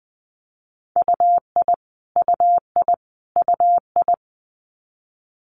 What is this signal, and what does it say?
Morse code, heard as a keyed tone, about 20 wpm, 700 Hz. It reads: UI UI UI
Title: Multi-letter Phonemes Repeated 3x 20wpm